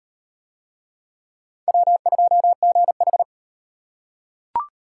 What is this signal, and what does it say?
Morse code, heard as a keyed tone, about 38 wpm, 700 Hz. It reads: W2GH